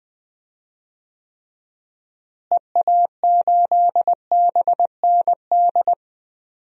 Morse code, heard as a keyed tone, about 21 wpm, 700 Hz. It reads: EA8BND
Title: DXCC-100 Call Signs - 6 Characters - Repeated 3x - 20wpm